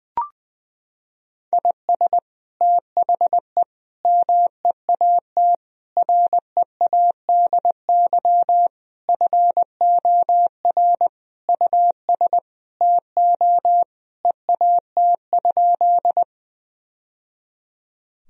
Morse code, heard as a keyed tone, about 21 wpm, 700 Hz. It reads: IS THE MEAT READY FOR US TO EAT?
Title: Sentences from Top 1000 Words - Review Part 3 - 20wpm